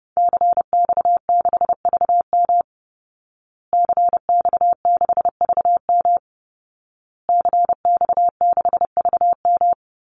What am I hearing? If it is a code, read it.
Morse code, heard as a keyed tone, about 30 wpm, 700 Hz. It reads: CX6VM CX6VM CX6VM
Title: Call Signs Repeated 2x - 30wpm